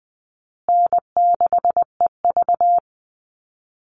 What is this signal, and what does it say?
Morse code, heard as a keyed tone, about 20 wpm, 700 Hz. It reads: N6EV
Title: Easy Call Signs Repeated 3x - 20wpm